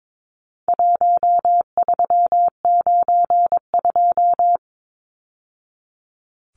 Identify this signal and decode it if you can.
Morse code, heard as a keyed tone, about 22 wpm, 700 Hz. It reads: 1392
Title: Numbers - Contest Style - 4-Digit 22wpm